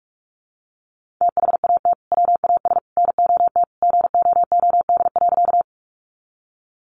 Morse code, heard as a keyed tone, about 47 wpm, 700 Hz. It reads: THAT WAS NOT GOOD.
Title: Sentences from Top 600 Words 45wpm